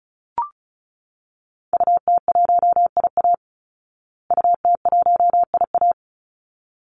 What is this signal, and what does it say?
Morse code, heard as a keyed tone, about 34 wpm, 700 Hz. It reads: UT1IA UT1IA
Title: Call Signs Repeated 3x - 35wpm